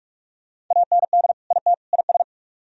Morse code, heard as a keyed tone, about 44 wpm, 700 Hz. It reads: AND IT IS